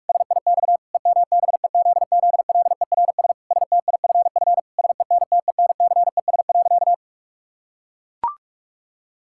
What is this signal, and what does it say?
Morse code, heard as a keyed tone, about 41 wpm, 700 Hz. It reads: SIX EMBEZZLERS STIFF SENTENCES.
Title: Pangram Challenge v2 - 45wpm